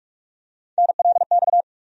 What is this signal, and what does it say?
Morse code, heard as a keyed tone, about 46 wpm, 700 Hz. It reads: NLX
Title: USA Call Sign Postfixes - 3 Letters 45wpm